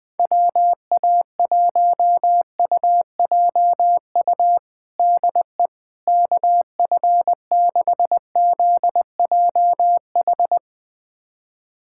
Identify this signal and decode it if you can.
Morse code, heard as a keyed tone, about 21 wpm, 700 Hz. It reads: WA1UJU DE KF6ZJH
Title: USA Callsign DE Callsign 20wpm